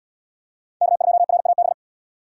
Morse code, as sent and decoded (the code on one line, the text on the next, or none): -.- ..--- .-. .- -...
K2RAB